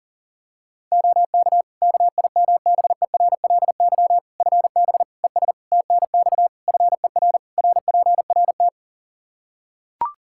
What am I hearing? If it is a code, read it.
Morse code, heard as a keyed tone, about 37 wpm, 700 Hz. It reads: OK KIMBERLY FB ES TNX FER RPRT
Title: Practice QSO - Exchange 2 - Info - 40wpm